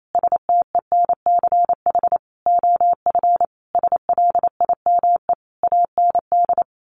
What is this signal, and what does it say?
Morse code, heard as a keyed tone, about 28 wpm, 700 Hz. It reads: STENCH OF SLIME AND
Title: A Sound Of Thunder 28wpm